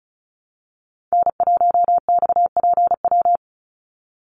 Morse code, heard as a keyed tone, about 36 wpm, 700 Hz. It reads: N1XPW
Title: USA Call Signs 35wpm